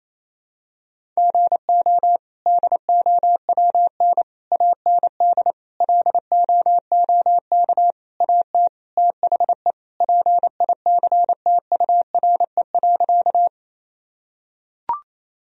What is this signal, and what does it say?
Morse code, heard as a keyed tone, about 28 wpm, 700 Hz. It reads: GO DOWN AND LOOK AT THE PICTURE.
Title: Sentences from Top 1000 Words - Review Part 1 - 28wpm